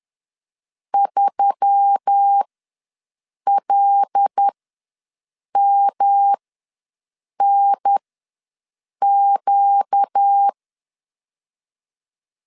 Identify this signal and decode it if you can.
Morse code, heard as a keyed tone, about 10 wpm, spaced as for 6 wpm, 800 Hz. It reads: 3LMNQ
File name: LEZIONI / 5 LBVC / 5 L ws 10wpm 3636